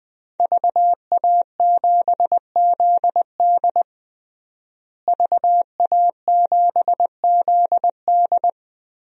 Morse code, heard as a keyed tone, about 20 wpm, 700 Hz. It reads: VA7ZD VA7ZD
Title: DXCC-100 Call Signs - 5 Characters - Repeated 3x - 20wpm